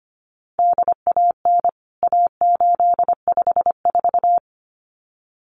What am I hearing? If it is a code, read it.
Morse code, heard as a keyed tone, about 25 wpm, 700 Hz. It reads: DAN A854
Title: CWT Exchange 25wpm